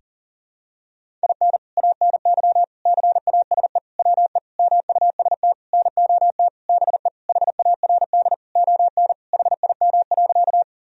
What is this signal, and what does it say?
Morse code, heard as a keyed tone, about 40 wpm, 700 Hz. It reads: IN ANY CASE WE MUST NOT BE HARD ON HIM.
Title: Sentences from Top 1000 Words - Review Part 1 - 40wpm